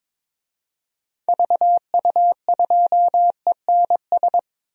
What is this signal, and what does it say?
Morse code, heard as a keyed tone, about 22 wpm, 700 Hz. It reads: VU2ENS
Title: DXCC-100 Call Signs - 6 Characters - Repeated 3x - 22wpm